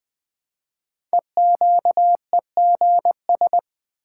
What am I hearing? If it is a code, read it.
Morse code, heard as a keyed tone, about 20 wpm, 700 Hz. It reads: EQEGS
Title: Code Groups - 5 Letters - 20wpm